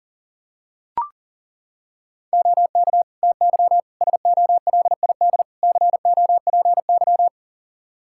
Morse code, heard as a keyed tone, about 40 wpm, 700 Hz. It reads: OK TY SOLID COPY
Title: K1USN SST QSO Practice 40wpm